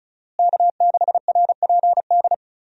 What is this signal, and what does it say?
Morse code, heard as a keyed tone, about 34 wpm, 700 Hz. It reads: K6RPD
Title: USA Call Signs - 5 Characters - Repeated 2x - 35wpm